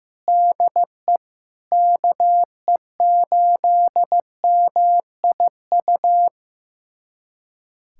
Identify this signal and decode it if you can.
Morse code, heard as a keyed tone, about 15 wpm, 700 Hz. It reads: DE KE8MIU
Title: USA Callsign DE Callsign - 6 Characters - 15wpm